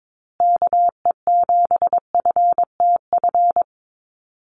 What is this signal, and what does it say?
Morse code, heard as a keyed tone, about 22 wpm, 700 Hz. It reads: KE7FTF
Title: USA Call Signs Repeated 2x 22wpm